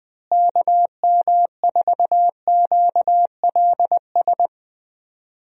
Morse code, heard as a keyed tone, about 20 wpm, 700 Hz. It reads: KM4QLS
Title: USA Call Signs - 6 Characters - Repeated 2x - 20wpm